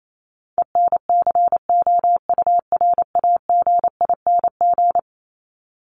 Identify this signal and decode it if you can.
Morse code, heard as a keyed tone, about 28 wpm, 700 Hz. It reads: ENCOURAGING
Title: Top 1751-2000 Words 28wpm